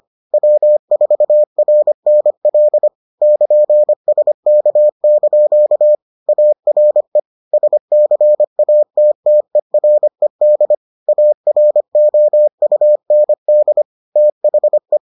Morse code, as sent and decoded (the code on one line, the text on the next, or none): .-- ....- .-. -. .-.. / -.--. ... -.- -.--.- / .- .-. / ... -.-. .- - - . .-. -.. / .- .-. --- ..- -. -.. / - .... .
W4RNL <KN>SK) ARE SCATTERED AROUND THE